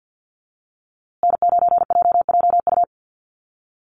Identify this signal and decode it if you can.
Morse code, heard as a keyed tone, about 49 wpm, 700 Hz. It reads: N9JJU
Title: USA Call Signs Repeated 3x 50wpm